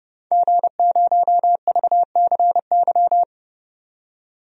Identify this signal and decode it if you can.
Morse code, heard as a keyed tone, about 31 wpm, 700 Hz. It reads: G0VCY
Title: DXCC-100 Call Signs - 5 Characters - 30wpm